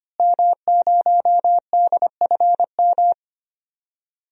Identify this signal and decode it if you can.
Morse code, heard as a keyed tone, about 25 wpm, 700 Hz. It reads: M0DFM
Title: DXCC-100 Call Signs - 5 Characters - Repeated 3x - 25wpm